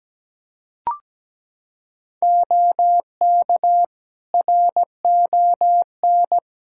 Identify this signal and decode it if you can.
Morse code, heard as a keyed tone, about 17 wpm, 700 Hz. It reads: OK RON
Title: Practice QSO - Exchange 4 - Ending - 17wpm